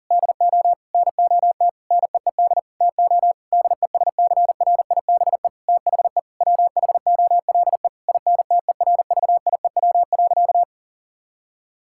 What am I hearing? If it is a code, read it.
Morse code, heard as a keyed tone, about 40 wpm, 700 Hz. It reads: DO NOT NEED TO DESCRIBE THE WHOLE INTERVIEW.
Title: Sentences from Top 1600 Words 40wpm